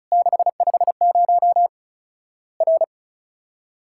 Morse code, heard as a keyed tone, about 36 wpm, 700 Hz. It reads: 650 R
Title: Worked All Europe DX Contest - QTC Exchange - 35wpm